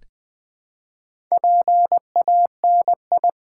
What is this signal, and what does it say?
Morse code, heard as a keyed tone, about 20 wpm, 700 Hz. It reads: PANI
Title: Build A Word - 1 Letter At A Time - 5-Letter Words - 20wpm